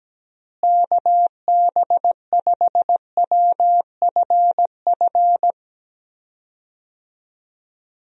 Morse code, heard as a keyed tone, about 17 wpm, 700 Hz. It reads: KB5WFF
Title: USA Call Signs - 6 Characters - Repeated 3x - 17wpm